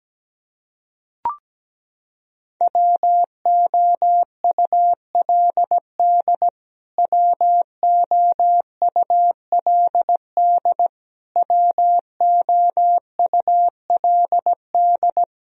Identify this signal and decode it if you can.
Morse code, heard as a keyed tone, about 17 wpm, 700 Hz. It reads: WOULD WOULD WOULD
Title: Top 100 Words Repeated 17wpm